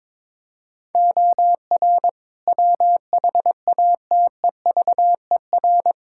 Morse code, heard as a keyed tone, about 21 wpm, 700 Hz. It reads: OR WHATEVER
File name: practice / 22wpm / Sets of 7 Words - 22wpm